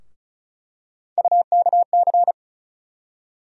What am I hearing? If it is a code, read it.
Morse code, heard as a keyed tone, about 36 wpm, 700 Hz. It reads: UKC
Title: USA Call Sign Postfixes - 3 Letters 35wpm